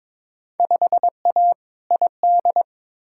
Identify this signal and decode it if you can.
Morse code, heard as a keyed tone, about 22 wpm, 700 Hz. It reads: 5A ID
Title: Field Day Exchanges - 22wpm